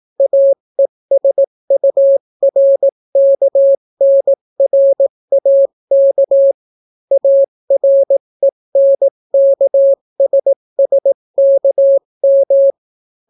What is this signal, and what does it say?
Morse code, heard as a keyed tone, about 18 wpm, 550 Hz. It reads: AESURKNRAK ARENKSSKM